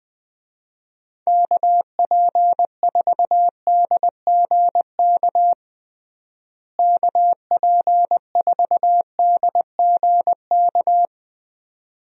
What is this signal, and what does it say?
Morse code, heard as a keyed tone, about 20 wpm, 700 Hz. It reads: KP4DGK KP4DGK
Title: USA Call Signs - 6 Characters - Repeated 2x - 20wpm